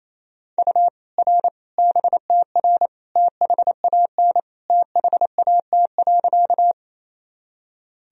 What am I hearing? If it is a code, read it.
Morse code, heard as a keyed tone, about 27 wpm, 700 Hz. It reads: U R BTR THAN THAT.
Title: Sentences from Top 400 Words and Abbreviations 28wpm